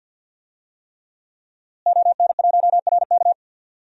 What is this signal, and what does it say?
Morse code, heard as a keyed tone, about 52 wpm, 700 Hz. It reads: ON1RK
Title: DXCC-100 Call Signs - 5 Characters - Repeated 3x - 50wpm